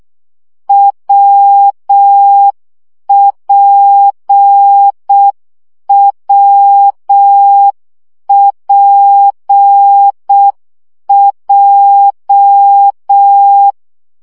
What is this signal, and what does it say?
Morse code, heard as a keyed tone, about 6 wpm, 800 Hz. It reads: WPWPJ
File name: CW gyakorlas w p j 5 baud